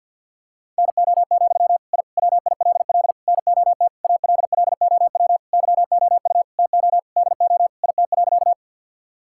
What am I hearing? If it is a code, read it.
Morse code, heard as a keyed tone, about 52 wpm, 700 Hz. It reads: NO, I WILL NOT ALLOW YOU TO DO IT.
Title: Sentences from Top 1000 Words - Review Part 5 - 50wpm